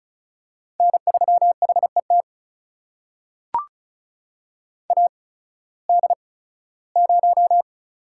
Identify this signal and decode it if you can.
Morse code, heard as a keyed tone, about 36 wpm, spaced as for 19 wpm, 700 Hz. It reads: N3HET AD0
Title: USA Call Signs - 8x Character Spacing - 35wpm